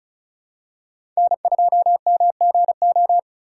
Morse code, heard as a keyed tone, about 36 wpm, 700 Hz. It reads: N2MGO
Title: USA Call Signs - 4x Character Spacing - 35wpm